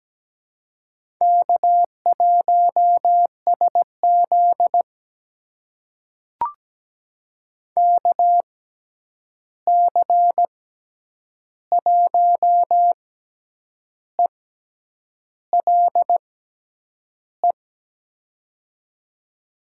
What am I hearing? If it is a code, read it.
Morse code, heard as a keyed tone, about 17 wpm, spaced as for 8 wpm, 700 Hz. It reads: K1SZ KC1ELE
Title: USA Call Signs - 6x Character Spacing - 17wpm